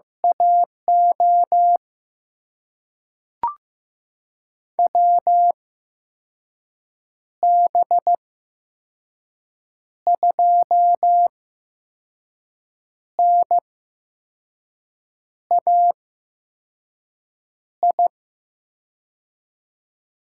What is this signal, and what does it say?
Morse code, heard as a keyed tone, about 15 wpm, spaced as for 5 wpm, 700 Hz. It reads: AO WB2NAI
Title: USA Call Signs - 6 Characters - 8x Character Spacing - 15wpm